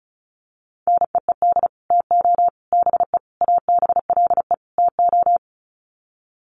Morse code, heard as a keyed tone, about 34 wpm, 700 Hz. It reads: NEED TO BE ABLE TO
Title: Sets of 5 Words Encore 35wpm